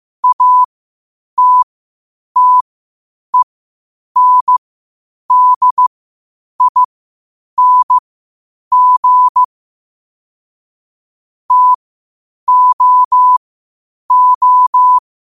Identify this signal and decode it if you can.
Morse code, heard as a keyed tone, about 15 wpm, spaced as for 8 wpm, 1000 Hz. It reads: ATTENDING TOO
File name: qotdfiles / 8wpm